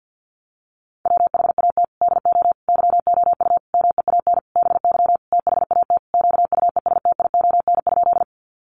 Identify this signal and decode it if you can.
Morse code, heard as a keyed tone, about 50 wpm, 700 Hz. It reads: WHAT DO YOU MEAN BY THAT QUESTION?
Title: Sentences from Top 400 Words 50wpm